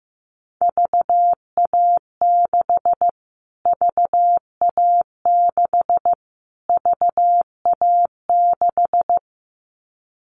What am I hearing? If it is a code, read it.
Morse code, heard as a keyed tone, about 15 wpm, 700 Hz. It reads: VA6 VA6 VA6
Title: Callsign Prefixes for Top 25 DXCC - Repeated 3x - 15wpm